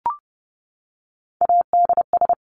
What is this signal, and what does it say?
Morse code, heard as a keyed tone, about 30 wpm, 700 Hz. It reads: ADS
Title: Abbreviations 30wpm